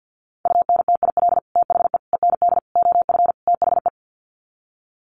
Morse code, heard as a keyed tone, about 50 wpm, 700 Hz. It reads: UNTIL THE END OF THE
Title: Sets of 5 Words Encore 50wpm